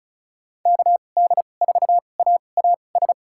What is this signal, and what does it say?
Morse code, heard as a keyed tone, about 34 wpm, spaced as for 25 wpm, 700 Hz. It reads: KD4AAS
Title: USA Call Signs - 6 Characters - 2x Character Spacing - 35wpm